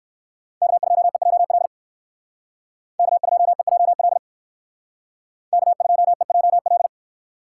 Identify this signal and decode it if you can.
Morse code, heard as a keyed tone, about 57 wpm, 700 Hz. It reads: K2EJL K2EJL K2EJL